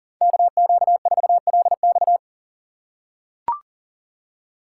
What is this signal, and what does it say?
Morse code, heard as a keyed tone, about 42 wpm, 700 Hz. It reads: KQ4LX
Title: USA Call Signs - 5 Characters - 6x Character Spacing - 40wpm